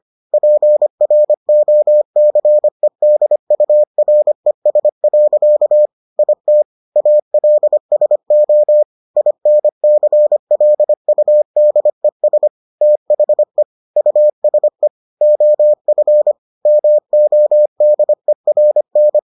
Morse code, heard as a keyed tone, about 25 wpm, 600 Hz. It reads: PROCEDURES. IT ALSO INCLUDES THE USE OF MODERN